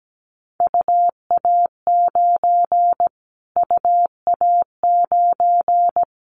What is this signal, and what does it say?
Morse code, heard as a keyed tone, about 17 wpm, 700 Hz. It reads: UA9 UA9
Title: Callsign Prefixes for Top 25 DXCC - Repeated 3x - 17wpm